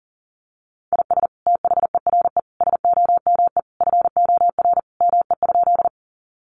Morse code, heard as a keyed tone, about 40 wpm, 700 Hz. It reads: IS THERE SOME FOR ME?
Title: Sentences from Top 200 Words 40wpm